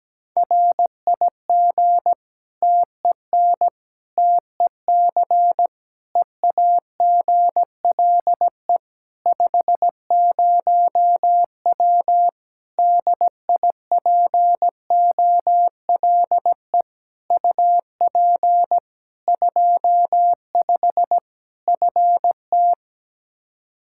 Morse code, heard as a keyed tone, about 17 wpm, 700 Hz. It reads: RIG TEN TEC EAGLE 50W DIPOLE UP 25 FT
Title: Practice QSO - Exchange 2 - Info - 17wpm